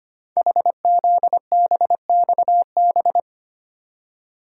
Morse code, heard as a keyed tone, about 25 wpm, 700 Hz. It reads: HZBXB
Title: Code Groups - 5 Letters - 25wpm